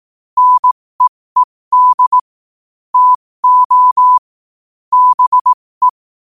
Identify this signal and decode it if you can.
Morse code, heard as a keyed tone, about 18 wpm, spaced as for 15 wpm, 1000 Hz. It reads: NEED TO BE
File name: qotdfiles / 15wpm